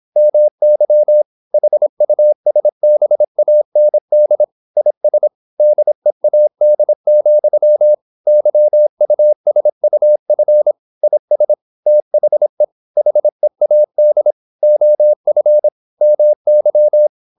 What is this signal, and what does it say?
Morse code, heard as a keyed tone, about 26 wpm, 600 Hz. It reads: MY HUSBAND IS DEAD, YUSUF IS THE HEAD OF MY
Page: Never 26wpm | CW med Gnister